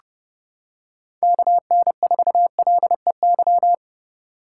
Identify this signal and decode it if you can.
Morse code, heard as a keyed tone, about 29 wpm, 700 Hz. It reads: KN4LEY